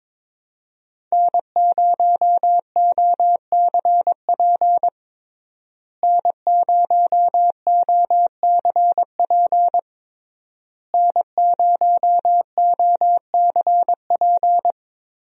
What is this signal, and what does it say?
Morse code, heard as a keyed tone, about 22 wpm, 700 Hz. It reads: N0OCP N0OCP N0OCP
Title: USA Call Signs - 5 Characters - Repeated 3x - 22wpm